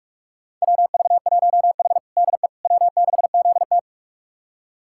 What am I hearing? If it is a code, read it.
Morse code, heard as a keyed tone, about 44 wpm, 700 Hz. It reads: WV1H DE W6ZT
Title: USA Callsign DE Callsign - 4 Characters - 45wpm